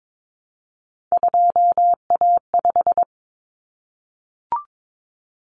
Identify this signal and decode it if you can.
Morse code, heard as a keyed tone, about 22 wpm, 700 Hz. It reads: 2A5
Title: Common Cut Numbers - 3-Digit 22wpm